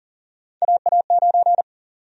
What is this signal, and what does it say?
Morse code, heard as a keyed tone, about 40 wpm, 700 Hz. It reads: AA9
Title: USA Call Sign Prefixes - Repeated 2x - 40wpm